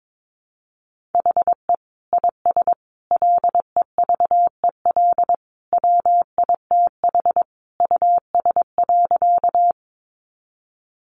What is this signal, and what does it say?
Morse code, heard as a keyed tone, about 22 wpm, 700 Hz. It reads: HE IS LEVEL WITH US.
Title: Sentences from Top 700 Words 22wpm